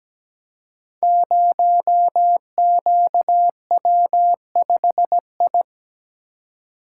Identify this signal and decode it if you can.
Morse code, heard as a keyed tone, about 17 wpm, 700 Hz. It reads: 0QW5I
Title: Code Groups - 5 Letter:Numbers - 17wpm